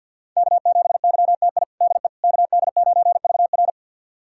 Morse code, heard as a keyed tone, about 51 wpm, 700 Hz. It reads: K7YTI DE KD9VR
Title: USA Callsign DE Callsign - 5 Characters - 50wpm